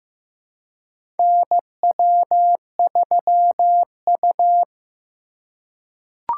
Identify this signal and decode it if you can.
Morse code, heard as a keyed tone, about 15 wpm, 700 Hz. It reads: NW3U
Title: USA Call Signs - 4 Characters - 2x Character Spacing - 15wpm